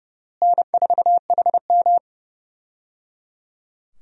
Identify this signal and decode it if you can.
Morse code, heard as a keyed tone, about 30 wpm, 700 Hz. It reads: N4HM
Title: USA Call Signs - 4 Characters - Repeated 3x - 30wpm